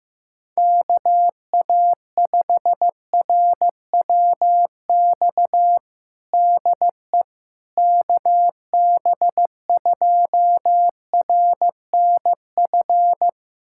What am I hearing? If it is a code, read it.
Morse code, heard as a keyed tone, about 15 wpm, 700 Hz. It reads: KA5RWX DE KB2RNF
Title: USA Callsign DE Callsign - 6 Characters - 15wpm